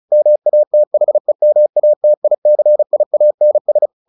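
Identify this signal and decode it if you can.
Morse code, heard as a keyed tone, about 34 wpm, 600 Hz. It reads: MATHEMATICIANS
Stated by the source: Morse practice at 35 wpm from CW med Gnister, sending Ken Follett's novel Never in English.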